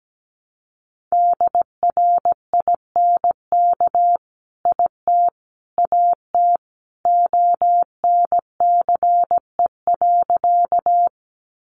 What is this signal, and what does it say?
Morse code, heard as a keyed tone, about 17 wpm, 700 Hz. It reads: DRINK IT AT ONCE.
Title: Sentences from Top 1500 Words 17wpm